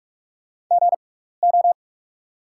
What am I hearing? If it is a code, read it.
Morse code, heard as a keyed tone, about 44 wpm, spaced as for 15 wpm, 700 Hz. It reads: GO
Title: 6x Character Spacing - Top 500 Words - 45wpm